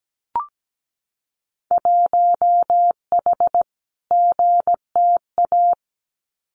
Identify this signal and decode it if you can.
Morse code, heard as a keyed tone, about 17 wpm, 700 Hz. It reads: 1H GTA
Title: Winter Field Day Exchanges 17wpm